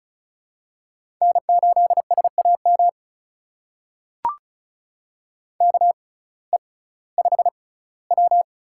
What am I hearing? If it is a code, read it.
Morse code, heard as a keyed tone, about 35 wpm, spaced as for 19 wpm, 700 Hz. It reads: N8SAM KE5W
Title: USA Call Signs - 5 Characters - 6x Character Spacing - 35wpm